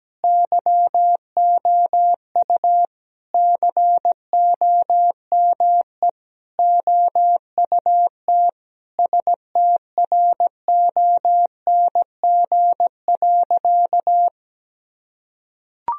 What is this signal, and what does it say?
Morse code, heard as a keyed tone, about 17 wpm, 700 Hz. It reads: YOU COME OUT STRONG.
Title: Sentences from Top 1000 Words - Review Part 3 - 17wpm